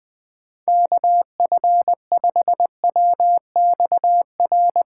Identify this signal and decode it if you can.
Morse code, heard as a keyed tone, about 20 wpm, 700 Hz. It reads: KF5WXR
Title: USA Call Signs - 6 Characters - 2x Character Spacing - 20wpm